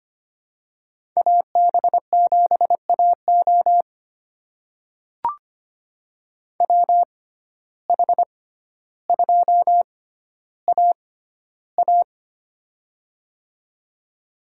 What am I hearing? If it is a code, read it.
Morse code, heard as a keyed tone, about 25 wpm, spaced as for 13 wpm, 700 Hz. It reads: AB7AO WH2AA
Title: USA Call Signs - 5 Characters - 6x Character Spacing - 25wpm